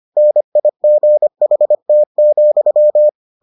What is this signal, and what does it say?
Morse code, heard as a keyed tone, about 25 wpm, 600 Hz. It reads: NIGHT,